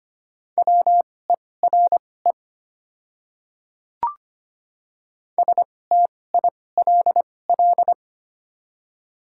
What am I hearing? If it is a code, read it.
Morse code, heard as a keyed tone, about 25 wpm, spaced as for 18 wpm, 700 Hz. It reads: WERE STILL
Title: 2x Character Spacing - Top 500 Words - 25wpm